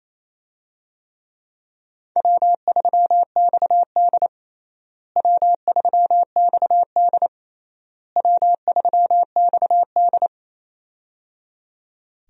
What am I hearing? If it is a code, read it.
Morse code, heard as a keyed tone, about 27 wpm, 700 Hz. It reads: W3XD W3XD W3XD
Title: USA Call Signs - 4 Characters - Repeated 3x - 28wpm